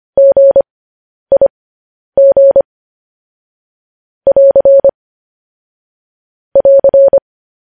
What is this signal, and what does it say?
Morse code, heard as a keyed tone, about 25 wpm, spaced as for 10 wpm, 550 Hz. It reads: GIG <AR> <AR>
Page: news-headlines-in-morse-code-at-10-wpm